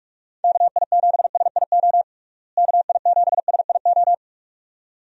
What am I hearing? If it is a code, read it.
Morse code, heard as a keyed tone, about 42 wpm, 700 Hz. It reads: KI7SIO KI7SIO
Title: USA Call Signs - 6 Characters - Repeated 2x - 45wpm